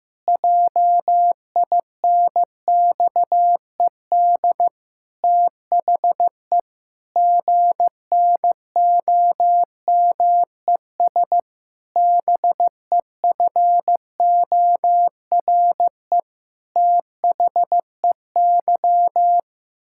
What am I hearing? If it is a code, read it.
Morse code, heard as a keyed tone, about 15 wpm, 700 Hz. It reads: JINXED THE GNOMES BEFORE THEY
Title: Pangram Challenge v2 - 15wpm